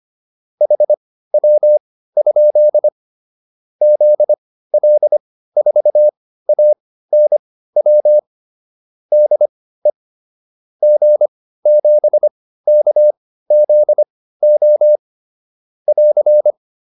Morse code, heard as a keyed tone, about 25 wpm, spaced as for 15 wpm, 600 Hz. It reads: HW? ZL4ANW DE G7KZO <AR>
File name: QSO404 15